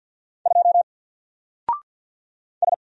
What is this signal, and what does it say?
Morse code, heard as a keyed tone, about 50 wpm, 700 Hz. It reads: 2 S
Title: Single Character (Letter:Number:Punctuation:Prosign) Alt 50wpm